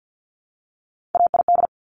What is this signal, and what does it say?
Morse code, heard as a keyed tone, about 50 wpm, 700 Hz. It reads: AID
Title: Top Three Letter Combinations in Words 50wpm